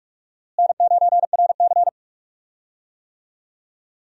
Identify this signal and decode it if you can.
Morse code, heard as a keyed tone, about 46 wpm, 700 Hz. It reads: N9RC
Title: USA Call Signs - 4 Characters - Repeated 3x - 45wpm